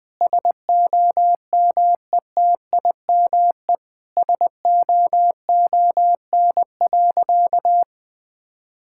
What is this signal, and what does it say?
Morse code, heard as a keyed tone, about 21 wpm, 700 Hz. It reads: SOMETIME SOON.
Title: Sentences from Top 1000 Words - Review Part 6 - 20wpm